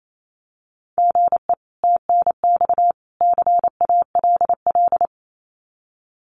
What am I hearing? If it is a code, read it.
Morse code, heard as a keyed tone, about 28 wpm, 700 Hz. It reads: GE TNX CALL
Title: Common QSO Phrases 28wpm